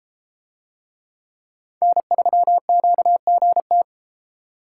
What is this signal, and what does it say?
Morse code, heard as a keyed tone, about 32 wpm, 700 Hz. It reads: N3QGT